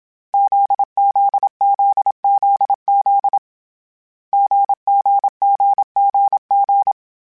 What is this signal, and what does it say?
Morse code, heard as a keyed tone, about 27 wpm, 800 Hz. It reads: ZZZZZ GGGGG
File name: Difficult Combos / Difficult Combos-mp3s / GZ GZ7 @25